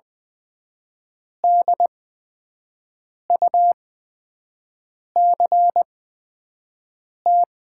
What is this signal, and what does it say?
Morse code, heard as a keyed tone, about 20 wpm, spaced as for 5 wpm, 700 Hz. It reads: DUCT
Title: 8x Character Spacing - Top 500 Words - 20wpm